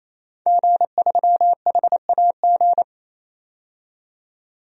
Morse code, heard as a keyed tone, about 28 wpm, 700 Hz. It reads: G3HAG
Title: DXCC-100 Call Signs - 5 Characters - 28wpm